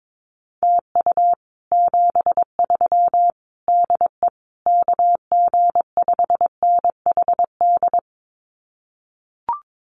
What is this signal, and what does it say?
Morse code, heard as a keyed tone, about 22 wpm, 700 Hz. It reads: TU 73 DE KG5NHD
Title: Common QSO Phrases Encore 22wpm